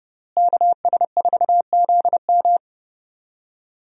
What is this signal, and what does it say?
Morse code, heard as a keyed tone, about 29 wpm, 700 Hz. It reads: KS4ZM
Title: USA Call Signs - 5 Characters - Repeated 2x - 30wpm